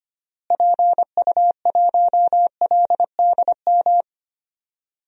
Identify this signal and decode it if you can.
Morse code, heard as a keyed tone, about 25 wpm, 700 Hz. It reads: PU1LDM
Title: DXCC-100 Call Signs - 6 Characters - Repeated 3x - 25wpm